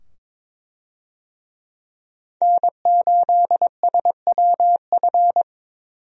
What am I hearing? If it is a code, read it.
Morse code, heard as a keyed tone, about 22 wpm, 700 Hz. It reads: N8SWF
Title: USA Call Signs - 5 Characters - 22wpm